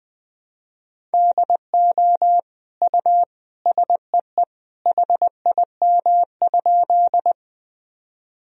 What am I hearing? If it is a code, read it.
Morse code, heard as a keyed tone, about 20 wpm, 700 Hz. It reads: DO U SEE HIM?
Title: Sentences from Top 100 Words and Abbreviations 20wpm